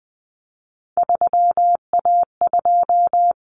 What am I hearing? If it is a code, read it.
Morse code, heard as a keyed tone, about 20 wpm, 700 Hz. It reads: 3A2
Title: Common Cut Numbers - 3-Digit 20wpm